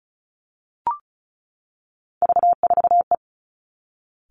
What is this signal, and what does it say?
Morse code, heard as a keyed tone, about 36 wpm, 700 Hz. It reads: V4E
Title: Three Challenging Letter-Number Combinations 35wpm